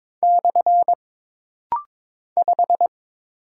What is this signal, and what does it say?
Morse code, heard as a keyed tone, about 22 wpm, 700 Hz. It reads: / 5
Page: This rapid-fire timing is geared to push your ICR (Instant Character Recognition) skills.